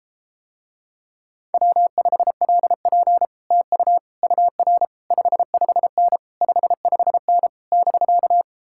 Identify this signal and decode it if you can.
Morse code, heard as a keyed tone, about 33 wpm, 700 Hz. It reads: W5LP TU UR 55N 55N <BK>